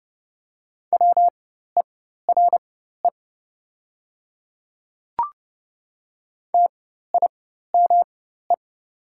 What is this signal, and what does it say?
Morse code, heard as a keyed tone, about 30 wpm, spaced as for 14 wpm, 700 Hz. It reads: WERE TIME
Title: 4x Character Spacing - Top 100 Words - 30wpm